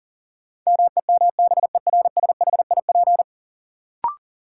Morse code, heard as a keyed tone, about 40 wpm, 700 Hz. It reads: MEMBERSHIP
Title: Sentences from Top 1400 Words - New Word Drill Alt - 40wpm